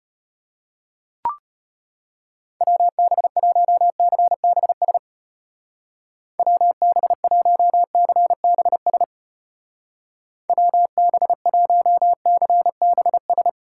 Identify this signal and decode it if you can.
Morse code, heard as a keyed tone, about 32 wpm, 700 Hz. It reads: WB1CBS WB1CBS WB1CBS